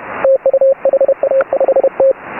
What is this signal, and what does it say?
Morse code, heard as a keyed tone, about 31 wpm, 540 Hz. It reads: TU HA5T